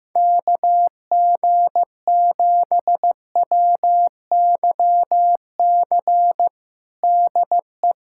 Morse code, heard as a keyed tone, about 15 wpm, 700 Hz. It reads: KG7WYC DE
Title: USA Callsign DE Callsign 15wpm